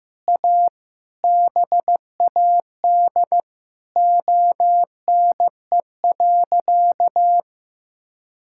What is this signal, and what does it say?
Morse code, heard as a keyed tone, about 15 wpm, 700 Hz. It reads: A BAD ONE.
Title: Sentences from Top 1000 Words - Review Part 6 - 15wpm